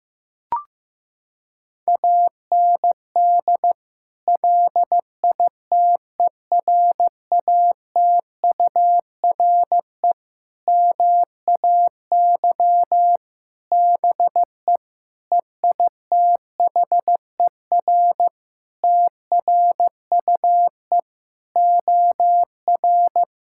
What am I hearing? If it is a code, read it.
Morse code, heard as a keyed tone, about 15 wpm, 700 Hz. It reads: AND LITERATURE MAY BE EITHER TRUE OR
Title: Sentences from Top 1600 Words 15wpm